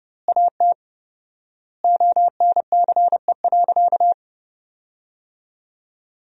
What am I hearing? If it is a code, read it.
Morse code, heard as a keyed tone, about 30 wpm, 700 Hz. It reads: AT ONCE.
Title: 4x Word-Spacing - Sentences from Top 300 Words - 30wpm